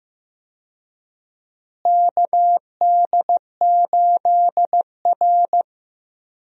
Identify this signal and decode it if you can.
Morse code, heard as a keyed tone, about 15 wpm, 700 Hz. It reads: KD8R
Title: USA Call Signs - 4 Characters - Repeated 3x - 15wpm